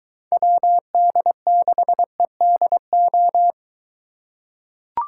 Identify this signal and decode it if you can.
Morse code, heard as a keyed tone, about 23 wpm, 700 Hz. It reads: WD6EDO